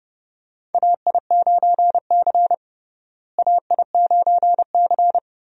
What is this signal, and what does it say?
Morse code, heard as a keyed tone, about 32 wpm, 700 Hz. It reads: AI9C AI9C
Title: USA Call Signs - 4 Characters - Repeated 3x - 30wpm